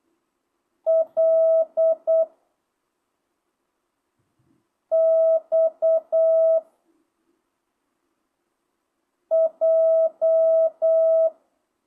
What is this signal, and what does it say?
Morse code, heard as a keyed tone, about 8 wpm, 660 Hz. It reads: L X J
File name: jh8cbh / 2025cw